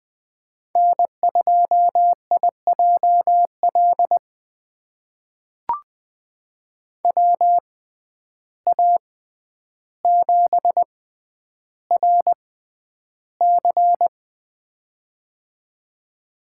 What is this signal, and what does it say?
Morse code, heard as a keyed tone, about 20 wpm, spaced as for 10 wpm, 700 Hz. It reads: N2IJL WA7RC
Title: USA Call Signs - 5 Characters - 6x Character Spacing - 20wpm